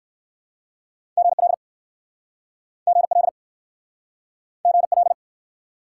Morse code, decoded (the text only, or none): GL GL GL